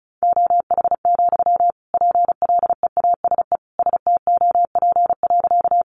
Morse code, heard as a keyed tone, about 34 wpm, 700 Hz. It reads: OH, PLEASE STOP.
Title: Sentences from Top 1500 Words 35wpm